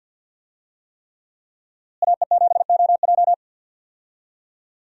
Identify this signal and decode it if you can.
Morse code, heard as a keyed tone, about 52 wpm, 700 Hz. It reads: AE7OJ
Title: USA Call Signs - 5 Characters - Repeated 3x - 50wpm